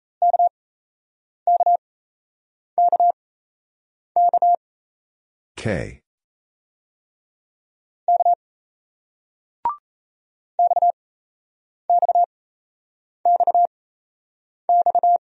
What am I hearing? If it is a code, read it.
Morse code, heard as a keyed tone, about 38 wpm, spaced as for 8 wpm, 700 Hz. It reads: KKKK K XXXX